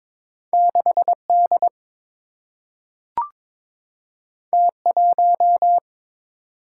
Morse code, heard as a keyed tone, about 22 wpm, 700 Hz. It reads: 6 D T 1
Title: Single-Letter Single-Number 22wpm